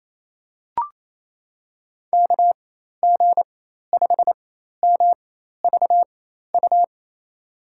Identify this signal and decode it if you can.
Morse code, heard as a keyed tone, about 28 wpm, spaced as for 13 wpm, 700 Hz. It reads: KG5MVU
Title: USA Call Signs - 4x Character Spacing - 28wpm